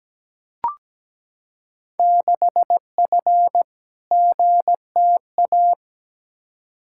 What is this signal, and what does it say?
Morse code, heard as a keyed tone, about 17 wpm, 700 Hz. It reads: 6F GTA
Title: Field Day Exchanges Encore - 17wpm